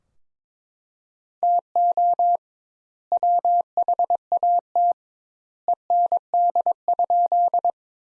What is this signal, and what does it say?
Morse code, heard as a keyed tone, about 22 wpm, 700 Hz. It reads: TO WHAT END?
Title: Sentences Top 200 Words - 22wpm